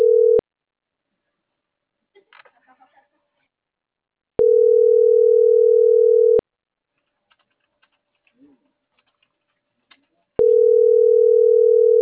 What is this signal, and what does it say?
Ringback tone (North America), 440 + 480 Hz.